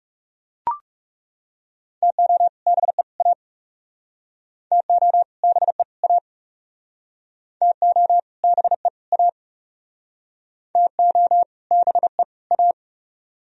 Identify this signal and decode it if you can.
Morse code, heard as a keyed tone, about 38 wpm, 700 Hz. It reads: TO BE A TO BE A TO BE A TO BE A